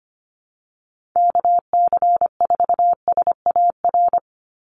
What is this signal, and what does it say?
Morse code, heard as a keyed tone, about 25 wpm, 700 Hz. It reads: KC4SAR
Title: USA Call Signs 25wpm